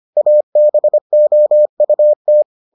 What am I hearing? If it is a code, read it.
Morse code, heard as a keyed tone, about 25 wpm, 600 Hz. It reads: ABOUT